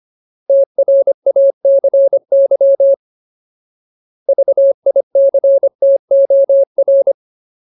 Morse code, heard as a keyed tone, about 25 wpm, 550 Hz. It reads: TRACY VICTOR
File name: Ultimate Word List / 25WPM Ultimate